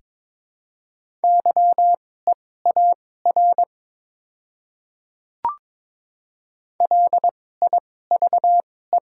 Morse code, heard as a keyed tone, about 22 wpm, spaced as for 16 wpm, 700 Hz. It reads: YEAR LIVE
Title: 2x Character Spacing - Top 200 Words - 22wpm